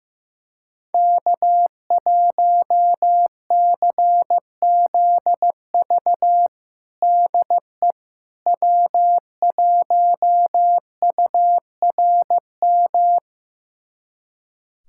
Morse code, heard as a keyed tone, about 15 wpm, 700 Hz. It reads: K1CZV DE W1URM
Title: USA Callsign DE Callsign - 5 Characters - 15wpm